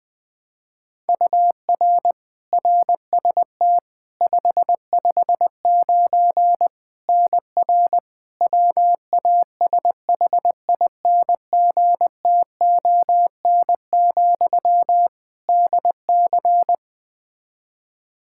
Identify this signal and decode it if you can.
Morse code, heard as a keyed tone, about 20 wpm, 700 Hz. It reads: UR RST 559 NR WASHINGTON, DC
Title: K1USN SST QSO Practice 20wpm